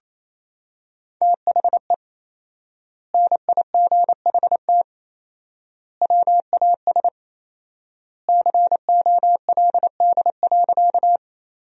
Morse code, heard as a keyed tone, about 28 wpm, 700 Hz. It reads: THE NIGHT WAS COLD.
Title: 4x Word-Spacing - Sentences from Top 400 Words - 28wpm